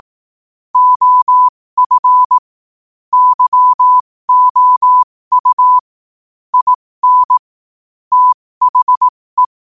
Morse code, heard as a keyed tone, about 18 wpm, spaced as for 15 wpm, 1000 Hz. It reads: OF YOU IN THE
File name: qotdfiles / 15wpm